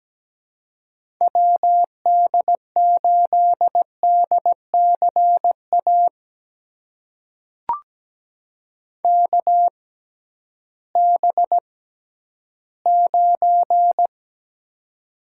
Morse code, heard as a keyed tone, about 17 wpm, spaced as for 11 wpm, 700 Hz. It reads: WD8DCA KB9
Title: USA Call Signs - 6 Characters - 6x Character Spacing - 17wpm